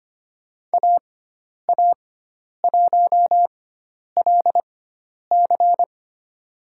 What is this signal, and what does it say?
Morse code, heard as a keyed tone, about 25 wpm, spaced as for 10 wpm, 700 Hz. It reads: AA1LC
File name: Lesson 16g - Letter c - Call Signs 25wpm